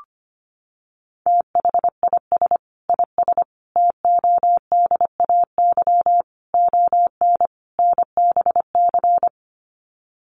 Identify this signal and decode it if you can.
Morse code, heard as a keyed tone, about 25 wpm, 700 Hz. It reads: THIS IS TODAY ON NBC
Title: Sets of 5 Words Encore 25wpm